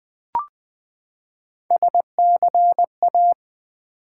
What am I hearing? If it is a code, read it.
Morse code, heard as a keyed tone, about 20 wpm, 700 Hz. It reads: SCA
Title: Build A Word - 3 Letters At A Time - 9-Letter Words - 20wpm